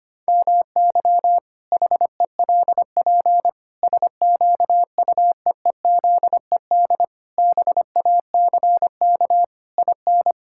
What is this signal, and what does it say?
Morse code, heard as a keyed tone, about 25 wpm, 700 Hz. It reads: MY HELP SQUEEZED BACK IN
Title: Pangram Challenge v2 - 25wpm